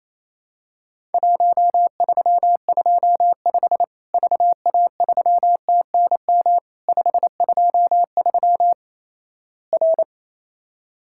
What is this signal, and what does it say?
Morse code, heard as a keyed tone, about 28 wpm, 700 Hz. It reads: 1325 VA3TNM 523 R
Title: Worked All Europe DX Contest - QTC Exchange - 28wpm